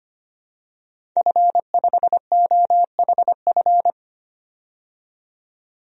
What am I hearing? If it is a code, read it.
Morse code, heard as a keyed tone, about 25 wpm, 700 Hz. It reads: F5OHF
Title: DXCC-100 Call Signs - 5 Characters - 25wpm